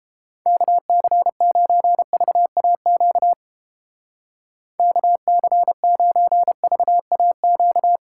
Morse code, heard as a keyed tone, about 31 wpm, 700 Hz. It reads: KC9VAQ KC9VAQ